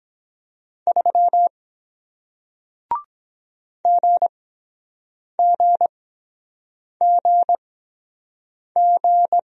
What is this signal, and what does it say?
Morse code, heard as a keyed tone, about 23 wpm, spaced as for 7 wpm, 700 Hz. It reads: 3 GGGG